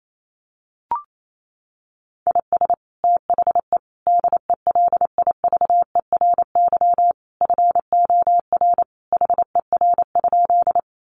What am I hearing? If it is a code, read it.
Morse code, heard as a keyed tone, about 28 wpm, 700 Hz. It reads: IS THE DELIVERY FOR HER?
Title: Sentences from Top 1500 Words 28wpm